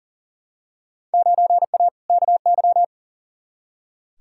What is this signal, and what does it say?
Morse code, heard as a keyed tone, about 40 wpm, 700 Hz. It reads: 9A KY
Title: Field Day Exchanges Encore - 40wpm